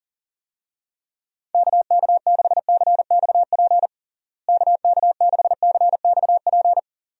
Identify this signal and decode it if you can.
Morse code, heard as a keyed tone, about 37 wpm, 700 Hz. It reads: KK6CXP KK6CXP
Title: USA Call Signs - 6 Characters - Repeated 2x - 40wpm